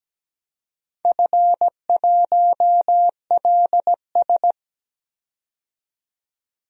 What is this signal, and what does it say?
Morse code, heard as a keyed tone, about 17 wpm, 700 Hz. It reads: F1LS
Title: DXCC-100 Call Signs - 4 Characters - 17wpm